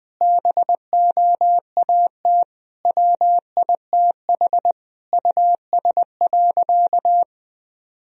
Morse code, heard as a keyed tone, about 21 wpm, 700 Hz. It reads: BOAT WITH US.
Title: Sentences from Top 1000 Words - Review Part 3 - 20wpm